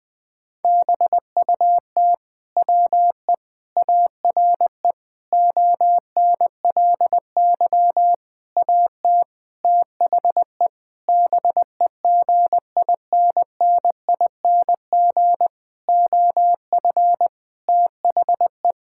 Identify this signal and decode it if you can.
Morse code, heard as a keyed tone, about 21 wpm, 700 Hz. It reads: BUT WE ARE ONLY AT THE BEGINNING OF THE
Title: Sentences from Top 1000 Words - Review Part 5 - 20wpm